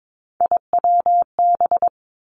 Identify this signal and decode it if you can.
Morse code, heard as a keyed tone, about 22 wpm, 700 Hz. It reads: IWB
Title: Three Challenging Letter-Number Combinations 22wpm